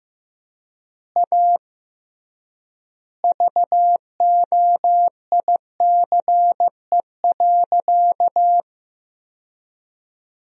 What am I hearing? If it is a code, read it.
Morse code, heard as a keyed tone, about 15 wpm, 700 Hz. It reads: A VOICE.
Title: 3x Word-Spacing - Sentences from Top 500 Words - 15wpm